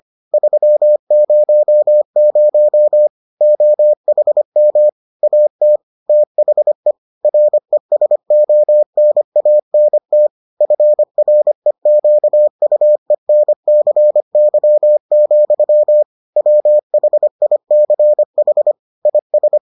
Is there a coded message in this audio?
Yes: Morse code from 0.3 to 19.6 s, heard as a keyed tone, about 25 wpm, 600 Hz. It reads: 300 OHM AT THE RESONANT FREQUENCY, WHICH IS